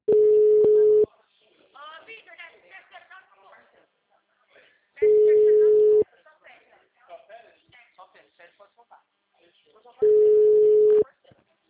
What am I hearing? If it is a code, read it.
Ringback tone (Europe, 425 Hz), with speech.